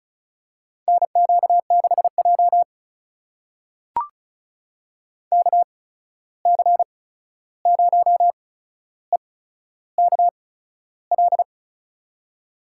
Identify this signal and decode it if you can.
Morse code, heard as a keyed tone, about 36 wpm, spaced as for 13 wpm, 700 Hz. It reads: NQ6J KC0EKL
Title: USA Call Signs - 8x Character Spacing - 35wpm